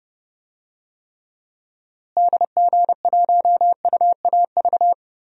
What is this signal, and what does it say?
Morse code, heard as a keyed tone, about 32 wpm, 700 Hz. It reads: DG1UAV
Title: DXCC-100 Call Signs - 6 Characters - Repeated 2x - 30wpm